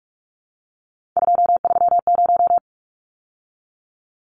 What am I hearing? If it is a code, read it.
Morse code, heard as a keyed tone, about 44 wpm, 700 Hz. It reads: 230
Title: Numbers - 3-Digit 45wpm